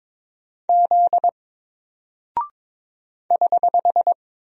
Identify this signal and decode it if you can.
Morse code, heard as a keyed tone, about 22 wpm, 700 Hz. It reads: Z <HH>
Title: Single Character (Letter:Number:Punctuation:Prosign) 22wpm